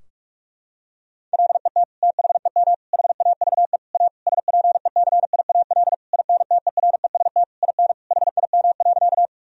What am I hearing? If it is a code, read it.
Morse code, heard as a keyed tone, about 46 wpm, 700 Hz. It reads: LET THEM HAVE A SPECIAL INTEREST IN HIM.
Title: Sentences from Top 1000 Words - Review Part 3 - 45wpm